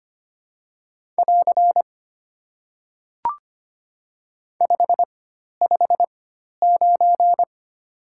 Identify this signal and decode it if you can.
Morse code, heard as a keyed tone, about 25 wpm, spaced as for 12 wpm, 700 Hz. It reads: <AR> 559
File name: Top 90 QSO Elements - 4x Character Spacing - 25wpm